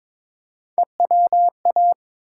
Morse code, heard as a keyed tone, about 22 wpm, 700 Hz. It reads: EWA
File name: ARRL Sections v2 - 22wpm